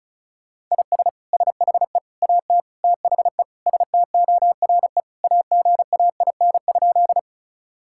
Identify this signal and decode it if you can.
Morse code, heard as a keyed tone, about 36 wpm, 700 Hz. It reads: IS SHE AT THE STORE AGAIN?
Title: Sentences from Top 1000 Words - Review Part 5 - 35wpm